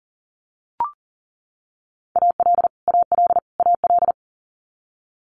Morse code, heard as a keyed tone, about 40 wpm, 700 Hz. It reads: AL AL AL
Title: Multi-letter Phonemes Repeated 3x 40wpm